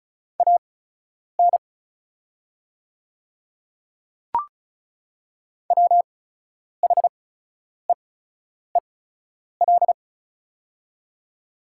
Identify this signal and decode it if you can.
Morse code, heard as a keyed tone, about 36 wpm, spaced as for 10 wpm, 700 Hz. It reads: AN WHEEL
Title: 8x Character Spacing - Top 500 Words - 35wpm